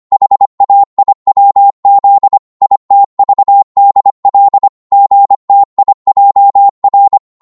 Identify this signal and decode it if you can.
Morse code, heard as a keyed tone, about 25 wpm, 800 Hz. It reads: HAIWZ ITVDL GTIJR
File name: Morse / ltr-25wpm